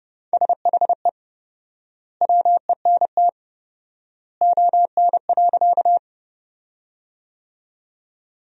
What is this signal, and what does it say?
Morse code, heard as a keyed tone, about 29 wpm, 700 Hz. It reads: SHE WENT ON.
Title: 4x Word-Spacing - Sentences from Top 200 Words - 30wpm